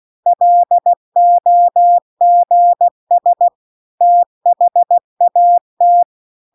Morse code, heard as a keyed tone, about 16 wpm, 700 Hz. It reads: LOGS THAT
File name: CW-The-Big-Twoheaded-River-2-16WPM